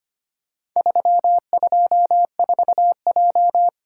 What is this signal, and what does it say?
Morse code, heard as a keyed tone, about 25 wpm, 700 Hz. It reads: 324J
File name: practice / 25wpm / Four Letter-Number Combinations - 25wpm